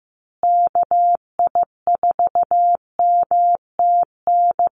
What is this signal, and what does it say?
Morse code, heard as a keyed tone, about 15 wpm, 700 Hz. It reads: KI4MTN
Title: USA Call Signs Repeated 2x 15wpm